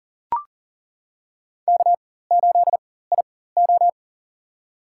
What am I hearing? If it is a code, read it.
Morse code, heard as a keyed tone, about 40 wpm, spaced as for 19 wpm, 700 Hz. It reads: K8IO
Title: USA Call Signs - 4 Characters - 4x Character Spacing - 40wpm